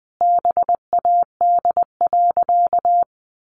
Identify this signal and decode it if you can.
Morse code, heard as a keyed tone, about 20 wpm, 700 Hz. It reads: BAD.
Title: Sentences from Top 700 Words 20wpm